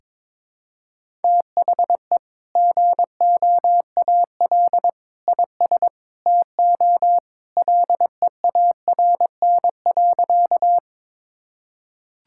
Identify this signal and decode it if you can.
Morse code, heard as a keyed tone, about 22 wpm, 700 Hz. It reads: THE GOAL IS TO LEARN.
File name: practice / 22wpm / Sentences with 2-Word Phrases - 22wpm